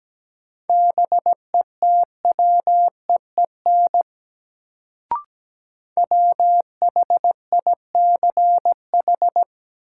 Morse code, heard as a keyed tone, about 17 wpm, 700 Hz. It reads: BETWEEN WHICH
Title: Sets of 2 Words - New Word Drill - 17wpm